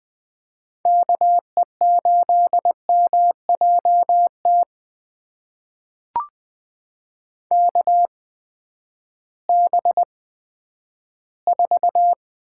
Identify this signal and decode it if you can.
Morse code, heard as a keyed tone, about 20 wpm, spaced as for 11 wpm, 700 Hz. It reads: KE8MJT KB4
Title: USA Call Signs - 6 Characters - 8x Character Spacing - 20wpm